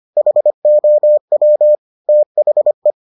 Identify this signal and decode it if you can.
Morse code, heard as a keyed tone, about 25 wpm, 600 Hz. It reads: HOW THE